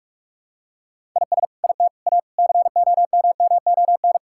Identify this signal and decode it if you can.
Morse code, heard as a keyed tone, about 46 wpm, 700 Hz. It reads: IS IT A COMMON
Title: Sentences from Top 1000 Words - Review Part 5 - 45wpm